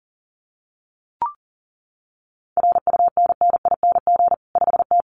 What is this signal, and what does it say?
Morse code, heard as a keyed tone, about 40 wpm, 700 Hz. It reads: RUNNING 5T
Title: Common QSO Phrases 40wpm